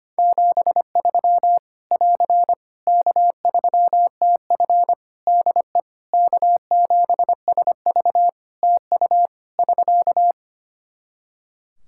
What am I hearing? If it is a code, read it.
Morse code, heard as a keyed tone, about 25 wpm, 700 Hz. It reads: 73 <AR> K3TF DE K7SV TU <SK>
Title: Practice QSO - Exchange 4 - Ending - 25wpm